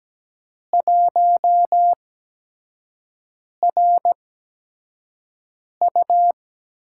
Morse code, heard as a keyed tone, about 17 wpm, spaced as for 5 wpm, 700 Hz. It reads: 1RU
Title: USA Call Signs - 6 Characters - 8x Character Spacing - 17wpm